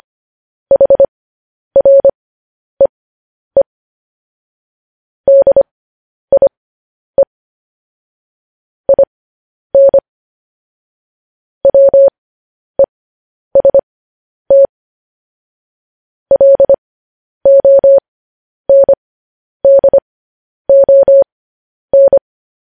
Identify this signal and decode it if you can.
Morse code, heard as a keyed tone, about 25 wpm, spaced as for 10 wpm, 550 Hz. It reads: HREE DIE IN WEST LONDON
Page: News Headlines in Morse Code at 10 WPM